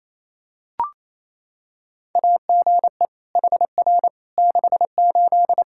Morse code, heard as a keyed tone, about 28 wpm, 700 Hz. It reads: AGE HR 68
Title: Practice QSO - Exchange 3 - More Info - 28wpm